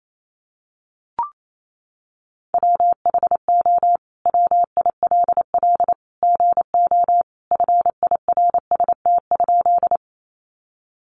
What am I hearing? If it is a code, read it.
Morse code, heard as a keyed tone, about 28 wpm, 700 Hz. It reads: WHO WILL GO FIRST?
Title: Sentences from Top 100 Words 28wpm